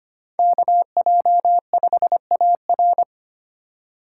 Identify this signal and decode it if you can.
Morse code, heard as a keyed tone, about 25 wpm, 700 Hz. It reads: KJ5AR
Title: USA Call Signs - 5 Characters - 2x Character Spacing - 25wpm